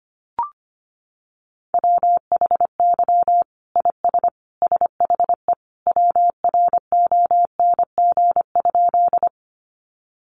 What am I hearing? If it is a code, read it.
Morse code, heard as a keyed tone, about 25 wpm, 700 Hz. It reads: WHY IS SHE WRONG?
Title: Sentences from Top 900 Words 25wpm